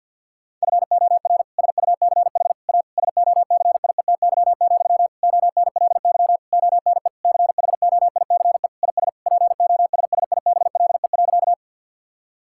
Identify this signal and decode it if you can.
Morse code, heard as a keyed tone, about 50 wpm, 700 Hz. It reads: FOR SUCH A SOCIETY, ONLY ONE CHOICE IS POSSIBLE.
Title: Sentences from Top 1000 Words - Review Part 6 - 50wpm